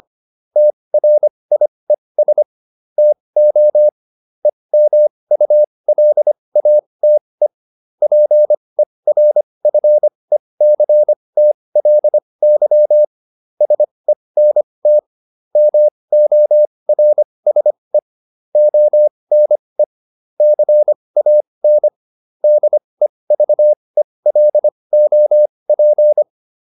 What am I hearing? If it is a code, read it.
Morse code, heard as a keyed tone, about 25 wpm, spaced as for 20 wpm, 600 Hz. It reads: TRIES TO EMULATE PERFECTLY SENT MORSE ONE CAN DEVELOP